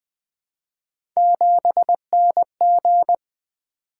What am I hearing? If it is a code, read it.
Morse code, heard as a keyed tone, about 20 wpm, 700 Hz. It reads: 7NG
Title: Three Letter-Number Combinations - 20wpm